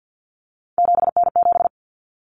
Morse code, heard as a keyed tone, about 50 wpm, 700 Hz. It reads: 7N7
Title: Common Cut Numbers - 3-Digit 50wpm